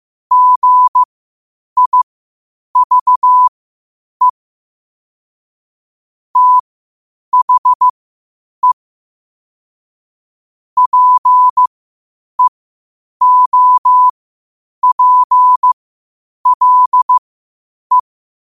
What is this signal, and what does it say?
Morse code, heard as a keyed tone, about 15 wpm, spaced as for 8 wpm, 1000 Hz. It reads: GIVE THE PEOPLE
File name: qotdfiles / 8wpm